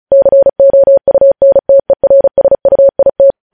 Morse code, heard as a keyed tone, about 36 wpm, 550 Hz. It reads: COUNTERSUIT
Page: news headlines in morse code at 35 wpm